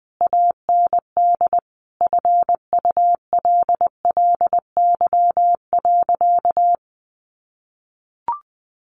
Morse code, heard as a keyed tone, about 20 wpm, 700 Hz. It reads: AND FULLY.
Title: Sentences from Top 1300 Words 20wpm